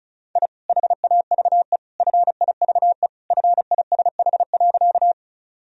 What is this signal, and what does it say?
Morse code, heard as a keyed tone, about 34 wpm, 700 Hz. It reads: I HAVE FIVE FISH.
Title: Sentences from Top 1000 Words - Review Part 2 - 35wpm